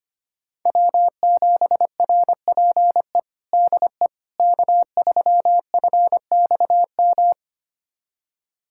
Morse code, heard as a keyed tone, about 25 wpm, 700 Hz. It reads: W7RPE DE K3FXM
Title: USA Callsign DE Callsign - 5 Characters - 25wpm